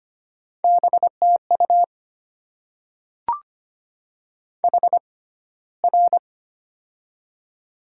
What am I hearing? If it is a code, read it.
Morse code, heard as a keyed tone, about 25 wpm, spaced as for 15 wpm, 700 Hz. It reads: BTU HR
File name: Top 90 QSO Elements - 6x Character Spacing - 25wpm